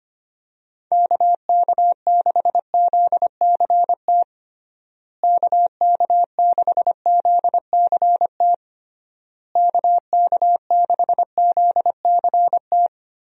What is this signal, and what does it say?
Morse code, heard as a keyed tone, about 25 wpm, 700 Hz. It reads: KK6ZCT KK6ZCT KK6ZCT
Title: USA Call Signs - 6 Characters - Repeated 3x - 25wpm